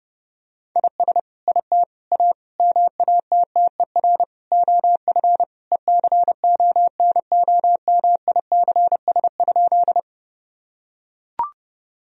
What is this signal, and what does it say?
Morse code, heard as a keyed tone, about 30 wpm, 700 Hz. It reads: IS IT A MATTER OF ECONOMICS?
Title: Sentences from Top 1000 Words - Review Part 3 - 30wpm